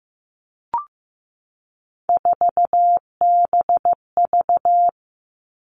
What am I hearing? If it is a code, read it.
Morse code, heard as a keyed tone, about 15 wpm, 700 Hz. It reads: 4BV
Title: Three Challenging Letter-Number Combinations 15wpm